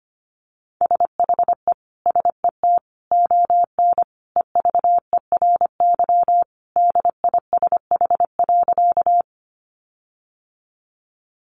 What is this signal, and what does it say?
Morse code, heard as a keyed tone, about 25 wpm, 700 Hz. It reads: SHE SET ON EVERY DISH.
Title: Sentences from Top 1300 Words 25wpm